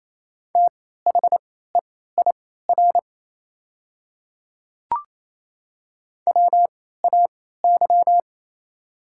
Morse code, heard as a keyed tone, about 27 wpm, spaced as for 16 wpm, 690 Hz. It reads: THEIR WAY
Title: 3x Character Spacing - Top 100 Words - 28wpm